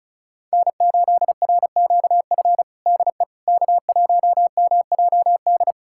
Morse code, heard as a keyed tone, about 36 wpm, 700 Hz. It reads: N8RQF DE K1MJD
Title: USA Callsign DE Callsign - 5 Characters - 35wpm